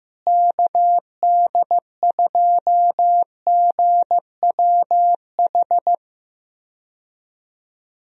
Morse code, heard as a keyed tone, about 15 wpm, 700 Hz. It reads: KD2GWH
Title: USA Call Signs - 6 Characters - Repeated 3x - 15wpm